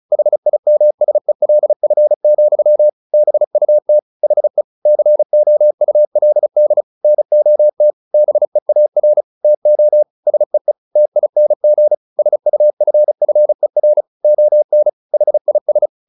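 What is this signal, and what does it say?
Morse code, heard as a keyed tone, about 34 wpm, 600 Hz. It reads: HIMSELF, BUT HE COULD NOT BEAR TO SEE TING SUFFER ON HIS
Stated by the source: Morse practice at 35 wpm from CW med Gnister, sending Ken Follett's novel Never in English.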